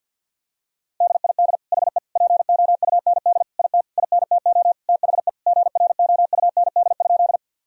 Morse code, heard as a keyed tone, about 50 wpm, 700 Hz. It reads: DID HE POUND IT INTO THE GROUND?
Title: Sentences from Top 1000 Words - Review Part 2 - 50wpm